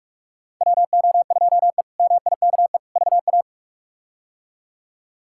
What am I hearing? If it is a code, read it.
Morse code, heard as a keyed tone, about 44 wpm, 700 Hz. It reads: WO2E MIKE VA
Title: K1USN SST QSO Practice - 45wpm